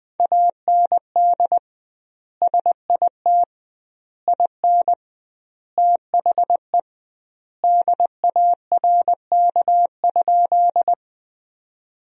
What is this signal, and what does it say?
Morse code, heard as a keyed tone, about 20 wpm, 700 Hz. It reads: AND SIT IN THE DARK?
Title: 2x Word-Spacing - Sentences from Top 500 Words - 20wpm